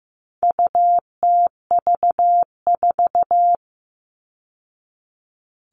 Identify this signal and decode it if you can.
Morse code, heard as a keyed tone, about 15 wpm, 700 Hz. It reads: UTV4
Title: All Cut Numbers - 4-Digit 15wpm